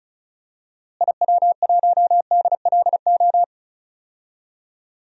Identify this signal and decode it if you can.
Morse code, heard as a keyed tone, about 36 wpm, 700 Hz. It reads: IW1DLO
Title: DXCC-100 Call Signs - 6 Characters - Repeated 2x - 35wpm